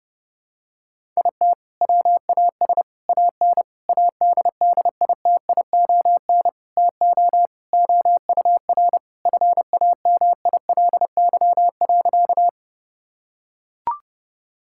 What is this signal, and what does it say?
Morse code, heard as a keyed tone, about 30 wpm, 700 Hz. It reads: IT WAS AN ADDITION TO OUR FAMILY.
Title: Sentences from Top 1000 Words - Review Part 6 - 30wpm